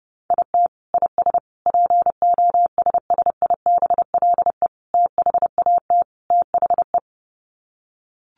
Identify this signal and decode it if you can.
Morse code, heard as a keyed tone, about 30 wpm, 700 Hz. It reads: IT IS POSSIBLE THAT THE
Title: Sets of 5 Words Encore2 30wpm